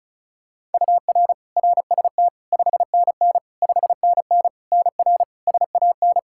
Morse code, heard as a keyed tone, about 34 wpm, 700 Hz. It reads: UR RST 5NN 5NN NR SAN
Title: Practice QSO - Exchange 1 - Beginning - 35wpm